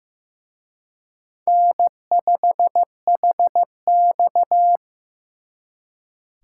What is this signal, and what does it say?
Morse code, heard as a keyed tone, about 18 wpm, 700 Hz. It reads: N5HX
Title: Speed Racing - USA Call Signs 15wpm